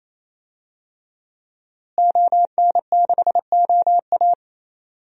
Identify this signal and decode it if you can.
Morse code, heard as a keyed tone, about 27 wpm, 700 Hz. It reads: ON6OA
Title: DXCC-100 Call Signs - 5 Characters - Repeated 3x - 28wpm